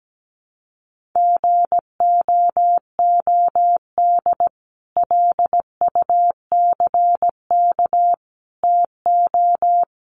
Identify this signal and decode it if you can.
Morse code, heard as a keyed tone, about 17 wpm, 700 Hz. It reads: GOOD LUCK TO
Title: Sentences from Top 1500 Words 17wpm